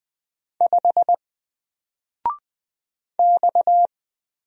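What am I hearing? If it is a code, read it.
Morse code, heard as a keyed tone, about 20 wpm, 700 Hz. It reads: 5 X
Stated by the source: Single letter and number Morse drill at 20 wpm, from N4LCD.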